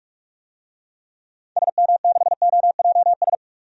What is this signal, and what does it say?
Morse code, heard as a keyed tone, about 45 wpm, 700 Hz. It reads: SM6OJS
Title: DXCC-100 Call Signs - 6 Characters - 45wpm